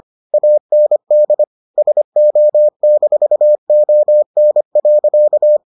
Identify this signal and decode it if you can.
Morse code, heard as a keyed tone, about 25 wpm, 600 Hz. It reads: AND SO-ON.